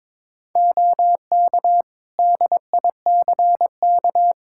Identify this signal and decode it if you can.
Morse code, heard as a keyed tone, about 22 wpm, 700 Hz. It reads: OK DICK
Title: Practice QSO - Exchange 4 - Ending - 22wpm